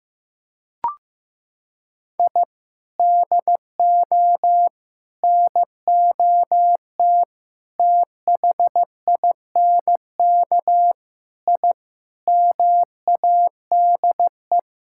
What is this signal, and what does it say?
Morse code, heard as a keyed tone, about 15 wpm, 700 Hz. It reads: I DO NOT THINK I MADE
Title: Sentences from Top 1000 Words - Review Part 6 - 15wpm